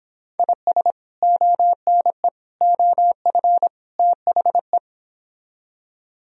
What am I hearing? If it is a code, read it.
Morse code, heard as a keyed tone, about 26 wpm, 700 Hz. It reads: IS ONE OF THE